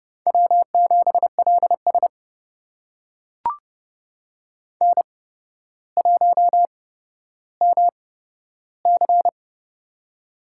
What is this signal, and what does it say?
Morse code, heard as a keyed tone, about 31 wpm, spaced as for 13 wpm, 700 Hz. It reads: W7LS N1MC
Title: USA Call Signs - 4 Characters - 8x Character Spacing - 30wpm